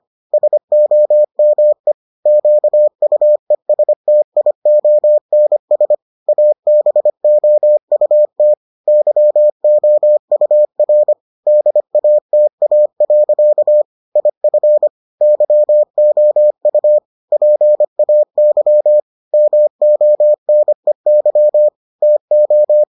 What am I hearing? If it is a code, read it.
Morse code, heard as a keyed tone, about 25 wpm, 600 Hz. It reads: SOME QUESTIONS ABOUT YOUR DATA. IF YOU PAY MONEY TO